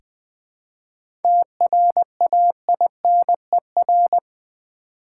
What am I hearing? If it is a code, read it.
Morse code, heard as a keyed tone, about 20 wpm, 700 Hz. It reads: TRAINER
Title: 7 Letter Words - First 3 Letters Given - 20wpm